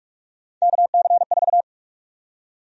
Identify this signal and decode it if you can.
Morse code, heard as a keyed tone, about 46 wpm, 700 Hz. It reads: KC4
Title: USA Call Sign Prefixes - Repeated 2x - 45wpm